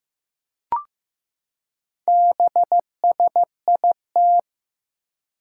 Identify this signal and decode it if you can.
Morse code, heard as a keyed tone, about 15 wpm, 700 Hz. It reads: BSIT
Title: Four Challenging Letter-Number Combinations 15wpm